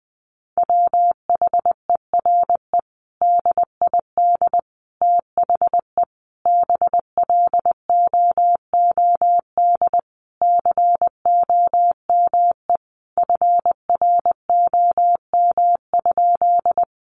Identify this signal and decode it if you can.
Morse code, heard as a keyed tone, about 20 wpm, 700 Hz. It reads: WHERE DID THE BLOOD COME FROM?
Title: Sentences from Top 1100 Words 20wpm